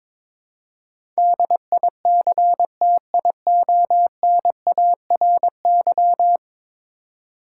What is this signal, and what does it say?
Morse code, heard as a keyed tone, about 22 wpm, 700 Hz. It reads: DICTIONARY
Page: morse-code-ninja-practice-22wpm